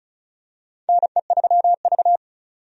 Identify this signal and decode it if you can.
Morse code, heard as a keyed tone, about 36 wpm, 700 Hz. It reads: NE3V
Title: USA Call Signs - 4 Characters - 35wpm